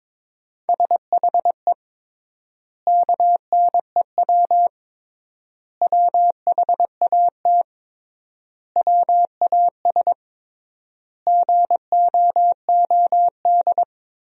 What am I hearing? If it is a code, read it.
Morse code, heard as a keyed tone, about 22 wpm, 700 Hz. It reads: SHE KNEW WHAT WAS GOOD
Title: 3x Word-Spacing - Sentences from Top 400 Words - 22wpm